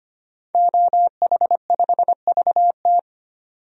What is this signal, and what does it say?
Morse code, heard as a keyed tone, about 25 wpm, 700 Hz. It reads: OH5VT
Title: DXCC-100 Call Signs - 5 Characters - 25wpm